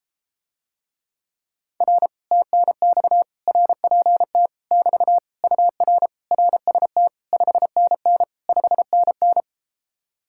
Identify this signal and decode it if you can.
Morse code, heard as a keyed tone, about 33 wpm, 700 Hz. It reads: R TNX RPT <BT> UR RST 5NN 5NN